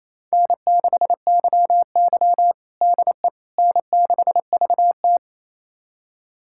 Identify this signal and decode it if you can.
Morse code, heard as a keyed tone, about 28 wpm, 700 Hz. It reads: N6YY DE N6VT
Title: USA Callsign DE Callsign - 4 Characters - 28wpm